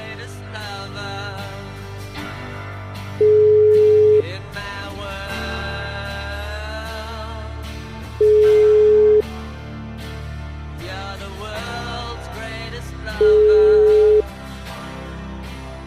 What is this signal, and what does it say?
Ringback tone (Europe, 425 Hz).